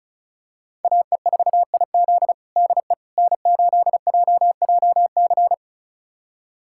Morse code, heard as a keyed tone, about 36 wpm, 700 Hz. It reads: AE4IZ DE N8JJC
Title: USA Callsign DE Callsign - 5 Characters - 35wpm